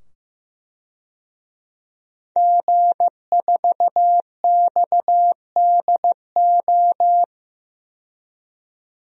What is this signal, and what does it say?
Morse code, heard as a keyed tone, about 15 wpm, 700 Hz. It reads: G4XDO
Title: DXCC-100 Call Signs - 5 Characters - 15wpm